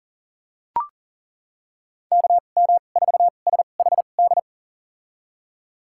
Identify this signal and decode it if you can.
Morse code, heard as a keyed tone, about 40 wpm, spaced as for 29 wpm, 700 Hz. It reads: KM4SHD
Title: USA Call Signs - 6 Characters - 2x Character Spacing - 40wpm